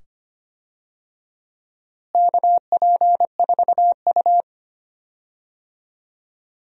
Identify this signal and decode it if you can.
Morse code, heard as a keyed tone, about 25 wpm, 700 Hz. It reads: KP4U
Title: DXCC-100 Call Signs - 4 Characters - Repeated 2x - 25wpm